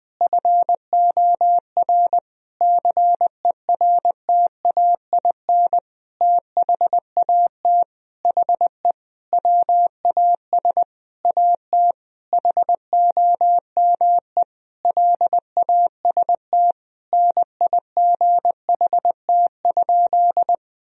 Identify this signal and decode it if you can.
Morse code, heard as a keyed tone, about 20 wpm, 700 Hz. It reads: FOR CERTAIN THAT HE WAS AT HOME LAST NIGHT?
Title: Sentences from Top 1000 Words - Review Part 2 - 20wpm